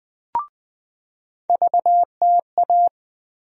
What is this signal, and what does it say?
Morse code, heard as a keyed tone, about 20 wpm, 700 Hz. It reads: VTA
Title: USA Call Sign Postfixes - 3 Letters 20wpm